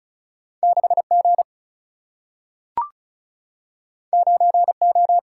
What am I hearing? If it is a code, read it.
Morse code, heard as a keyed tone, about 36 wpm, 700 Hz. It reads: 6 G 9 O
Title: Single-Letter Single-Number 35wpm